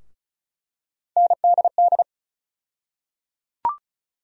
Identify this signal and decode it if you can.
Morse code, heard as a keyed tone, about 34 wpm, 700 Hz. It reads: NDD
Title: USA Call Sign Postfixes - 3 Letters 35wpm